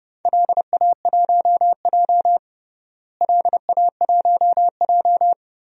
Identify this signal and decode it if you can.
Morse code, heard as a keyed tone, about 32 wpm, 700 Hz. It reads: LA1J LA1J
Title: DXCC-100 Call Signs - 4 Characters - Repeated 3x - 30wpm